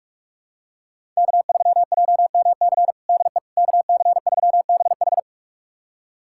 Morse code, heard as a keyed tone, about 42 wpm, 700 Hz. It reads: K3JMC DE KC3BH
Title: USA Callsign DE Callsign - 5 Characters - 45wpm